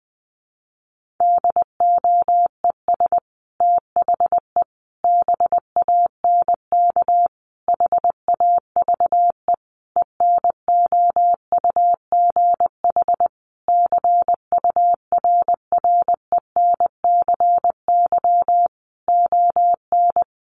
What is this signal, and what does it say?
Morse code, heard as a keyed tone, about 20 wpm, 700 Hz. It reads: DOES THE BANK HAVE ENOUGH CURRENCY ON
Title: Sentences from Top 1500 Words 20wpm